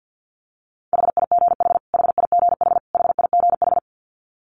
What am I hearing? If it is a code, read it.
Morse code, heard as a keyed tone, about 50 wpm, 700 Hz. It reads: HIGH HIGH HIGH
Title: Top 300 Words Repeated 50wpm